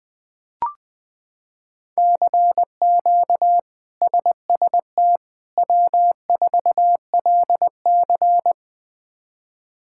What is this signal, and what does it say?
Morse code, heard as a keyed tone, about 20 wpm, 700 Hz. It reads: CQ SST W4LC
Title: K1USN SST QSO Practice - 20wpm